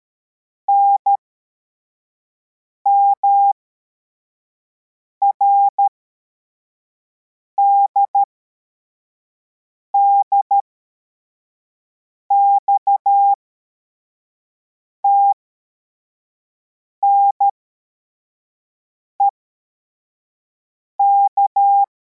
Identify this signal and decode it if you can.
Morse code, heard as a keyed tone, about 13 wpm, spaced as for 4 wpm, 800 Hz. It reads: NMRDD XTNEK
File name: formation / cw / Text25